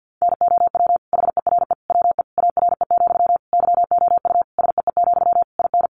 Morse code, heard as a keyed tone, about 49 wpm, 700 Hz. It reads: NOW HERE WE ARE, YOU SEE, IN
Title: Sentences from Top 1200 Words 50wpm